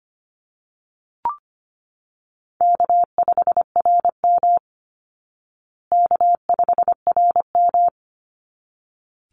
Morse code, heard as a keyed tone, about 25 wpm, 700 Hz. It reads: K5RM K5RM
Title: Easy Call Signs Repeated 2x - 25wpm